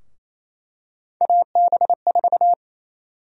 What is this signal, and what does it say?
Morse code, heard as a keyed tone, about 28 wpm, 700 Hz. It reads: AB4
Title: USA Call Sign Prefixes - 28wpm